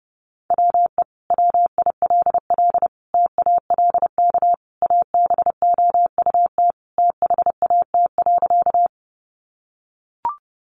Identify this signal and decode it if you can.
Morse code, heard as a keyed tone, about 30 wpm, 700 Hz. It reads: WE WILL TALK ABOUT THAT.
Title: Sentences from Top 400 Words 30wpm